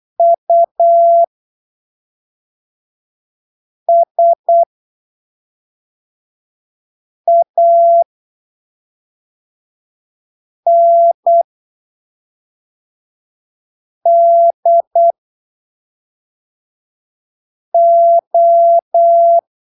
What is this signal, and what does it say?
Morse code, heard as a keyed tone, about 8 wpm, spaced as for 3 wpm, 680 Hz. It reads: USANDO